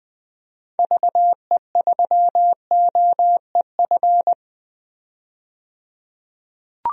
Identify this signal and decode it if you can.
Morse code, heard as a keyed tone, about 21 wpm, 700 Hz. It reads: VE3OEF
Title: DXCC-100 Call Signs - 6 Characters - 20wpm